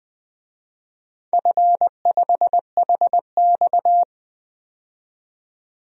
Morse code, heard as a keyed tone, about 20 wpm, 700 Hz. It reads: F5HX
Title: DXCC-100 Call Signs - 4 Characters - 20wpm